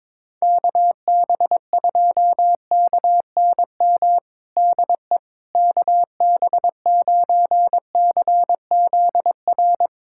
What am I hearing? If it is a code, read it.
Morse code, heard as a keyed tone, about 22 wpm, 700 Hz. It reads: KB2KNM DE KB9CZR
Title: USA Callsign DE Callsign - 6 Characters - 22wpm